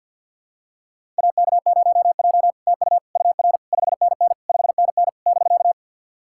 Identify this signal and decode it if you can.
Morse code, heard as a keyed tone, about 50 wpm, 700 Hz. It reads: AK0J TU UR 5NN 5NN <BK>
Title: SOTA QSO Practice 50wpm